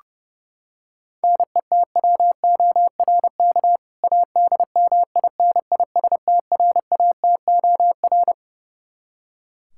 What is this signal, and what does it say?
Morse code, heard as a keyed tone, about 30 wpm, 700 Hz. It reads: NETWORK ADMINISTRATOR
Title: K1USN SST QSO Practice 30wpm